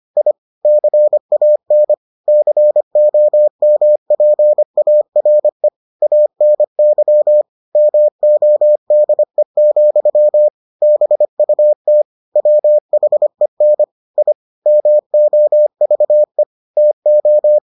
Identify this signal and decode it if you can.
Morse code, heard as a keyed tone, about 25 wpm, 600 Hz. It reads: I CAN COMPARE ANY MODE, BUT WHEN I MOVE TO